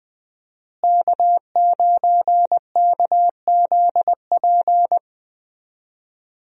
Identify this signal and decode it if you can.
Morse code, heard as a keyed tone, about 21 wpm, 700 Hz. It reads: K9KZP
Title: USA Call Signs - 5 Characters - Repeated 3x - 20wpm